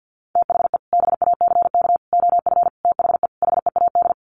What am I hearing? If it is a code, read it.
Morse code, heard as a keyed tone, about 48 wpm, 700 Hz. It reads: THE BACK OF THE HEAD
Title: Sets of 5 Words Encore2 50wpm